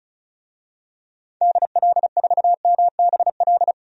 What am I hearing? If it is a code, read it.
Morse code, heard as a keyed tone, about 36 wpm, 700 Hz. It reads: DL4MBL
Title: DXCC-100 Call Signs - 6 Characters - 35wpm